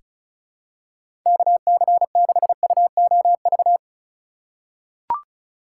Morse code, heard as a keyed tone, about 34 wpm, 700 Hz. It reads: KC6UOV
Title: USA Call Signs - 3x Character Spacing - 35wpm